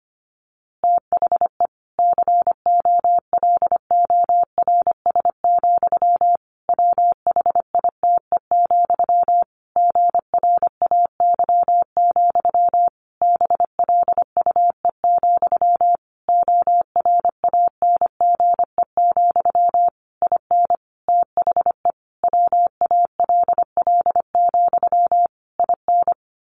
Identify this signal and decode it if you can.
Morse code, heard as a keyed tone, about 25 wpm, 700 Hz. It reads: THE COLORS, WHITE, GRAY, BLUE, ORANGE, IN THE WALL, IN
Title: A Sound Of Thunder 25wpm